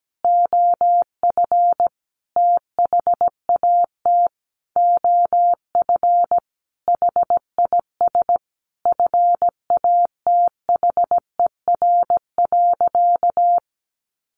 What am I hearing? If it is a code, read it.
Morse code, heard as a keyed tone, about 17 wpm, 700 Hz. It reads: OF THAT OF HIS FATHER.
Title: Sentences from Top 1300 Words 17wpm